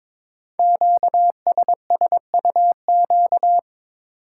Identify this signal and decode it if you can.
Morse code, heard as a keyed tone, about 22 wpm, 700 Hz. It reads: QSSUQ
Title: Code Groups - 5 Letters - 22wpm